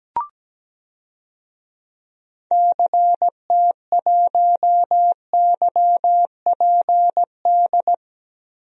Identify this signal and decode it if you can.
Morse code, heard as a keyed tone, about 17 wpm, 700 Hz. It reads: CT1YPD
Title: DXCC-100 Call Signs - 6 Characters - 17wpm